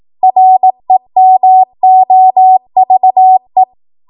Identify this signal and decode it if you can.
Morse code, heard as a keyed tone, about 18 wpm, 750 Hz. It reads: REMOVE
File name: Morse / 18WPM